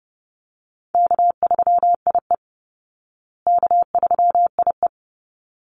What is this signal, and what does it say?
Morse code, heard as a keyed tone, about 30 wpm, 700 Hz. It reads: K3IE K3IE
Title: Call Signs Repeated 3x - 30wpm